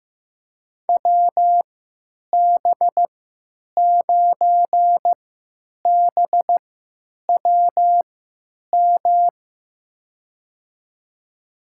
Morse code, heard as a keyed tone, about 15 wpm, spaced as for 9 wpm, 700 Hz. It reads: WB9BWM
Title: USA Call Signs - 6 Characters - 3x Character Spacing - 15wpm